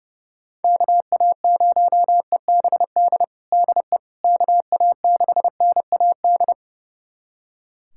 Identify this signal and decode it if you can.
Morse code, heard as a keyed tone, about 30 wpm, 700 Hz. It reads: KA0EBD DE KA6NAD
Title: USA Callsign DE Callsign - 6 Characters - 30wpm